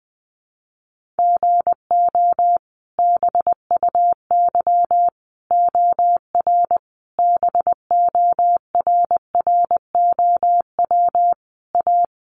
Morse code, heard as a keyed tone, about 20 wpm, 700 Hz. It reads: GO BUY OR BORROW A
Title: Sentences from Top 1500 Words 20wpm